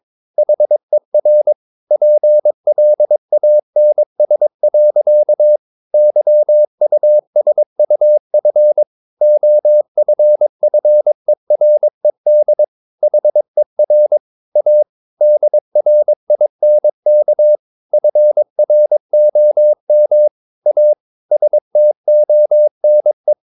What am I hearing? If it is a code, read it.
Morse code, heard as a keyed tone, about 22 wpm, 600 Hz. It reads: HER PLANS. YUSUF OFFERED HER A DRINK FROM A STONE
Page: Romanen Never af Ken Follet på engelsk.